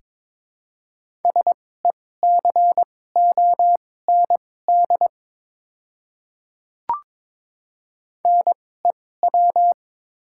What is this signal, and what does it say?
Morse code, heard as a keyed tone, about 22 wpm, spaced as for 16 wpm, 700 Hz. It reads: SECOND NEW
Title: 2x Character Spacing - Top 500 Words - 22wpm